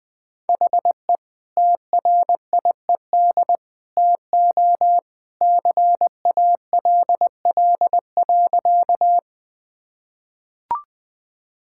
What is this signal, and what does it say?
Morse code, heard as a keyed tone, about 20 wpm, 700 Hz. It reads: HE TRIED TO CALL.
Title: Sentences with 2-Word Phrases - 20wpm